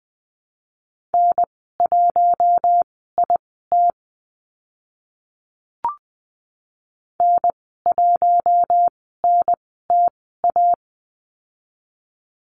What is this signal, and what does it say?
Morse code, heard as a keyed tone, about 20 wpm, spaced as for 15 wpm, 700 Hz. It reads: N1IT N1NTA
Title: 042 Lesson 8g - Number 1 - Call Signs 20wpm